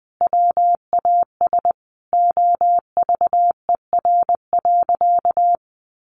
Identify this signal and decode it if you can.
Morse code, heard as a keyed tone, about 20 wpm, 700 Hz. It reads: WAS OVER.
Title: Sentences from Top 1200 Words 20wpm